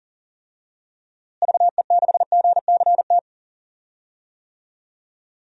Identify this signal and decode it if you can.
Morse code, heard as a keyed tone, about 37 wpm, 700 Hz. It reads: VE6GCT
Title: DXCC-100 Call Signs - 6 Characters - Repeated 2x - 40wpm